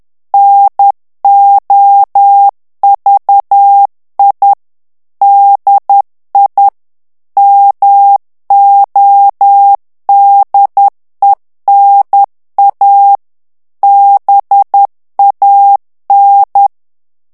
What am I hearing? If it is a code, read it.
Morse code, heard as a keyed tone, about 10 wpm, 800 Hz. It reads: NOVI DI MODENA BAN
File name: morse mp3 / ITALIEN texte clair 1 vitesse 10 mots minute